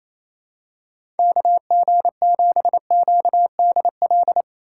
Morse code, heard as a keyed tone, about 28 wpm, 700 Hz. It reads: KG7QDL
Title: USA Call Signs - 6 Characters - 3x Character Spacing - 28wpm